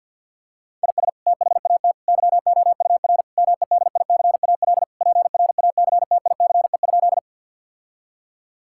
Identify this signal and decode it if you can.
Morse code, heard as a keyed tone, about 49 wpm, 700 Hz. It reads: IS THAT YOUR MEDICAL PRACTICE?
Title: Sentences from Top 1000 Words - Review Part 5 - 50wpm